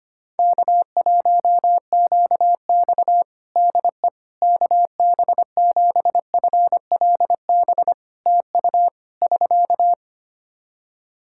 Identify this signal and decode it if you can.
Morse code, heard as a keyed tone, about 25 wpm, 700 Hz. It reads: K1QX DE KB7FLB TU <SK>
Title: Practice QSO - Exchange 4 - Ending - 25wpm